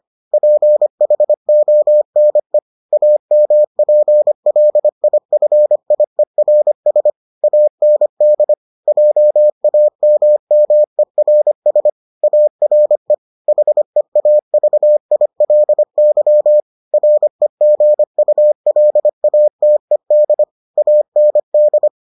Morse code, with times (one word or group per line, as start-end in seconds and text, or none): 0.3-2.6 PHONE
2.9-7.1 AMPLIFIERS
7.4-8.5 AND
8.9-11.9 JAMMERS
12.2-13.1 ARE
13.5-16.6 HEAVILY
16.9-20.4 REGULATED
20.8-21.9 AND